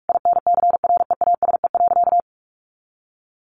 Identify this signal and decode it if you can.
Morse code, heard as a keyed tone, about 44 wpm, 700 Hz. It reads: INCREASE.
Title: Sentences from Top 1400 Words 45wpm